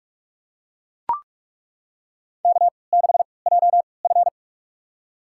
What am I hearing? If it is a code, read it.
Morse code, heard as a keyed tone, about 48 wpm, spaced as for 26 wpm, 690 Hz. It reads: K6JF
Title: USA Call Signs - 4 Characters - 3x Character Spacing - 45wpm